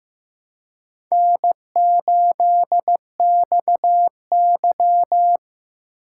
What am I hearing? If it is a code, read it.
Morse code, heard as a keyed tone, about 15 wpm, 700 Hz. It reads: N8XY
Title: USA Call Signs - 4x Character Spacing - 15wpm